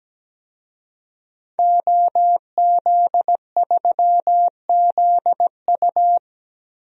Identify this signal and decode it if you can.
Morse code, heard as a keyed tone, about 17 wpm, 700 Hz. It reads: OZ3ZU
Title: DXCC-100 Call Signs - 5 Characters - 17wpm